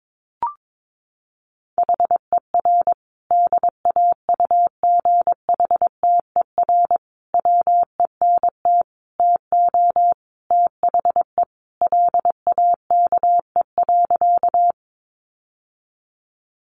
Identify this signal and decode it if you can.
Morse code, heard as a keyed tone, about 22 wpm, 700 Hz. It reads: HER DAUGHTER WENT TO THE LAKE.
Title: Sentences from Top 1500 Words 22wpm